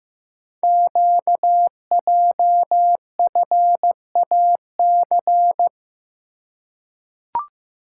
Morse code, heard as a keyed tone, about 15 wpm, 700 Hz. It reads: QJFAC
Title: Code Groups - 5 Letters - 15wpm